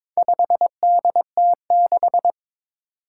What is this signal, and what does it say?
Morse code, heard as a keyed tone, about 22 wpm, 700 Hz. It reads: 5DT6
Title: Four Challenging Letter-Number Combinations 22wpm